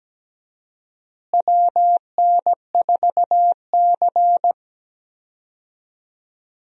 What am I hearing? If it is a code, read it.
Morse code, heard as a keyed tone, about 17 wpm, 700 Hz. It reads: WN4C
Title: USA Call Signs - 4 Characters - Repeated 3x - 17wpm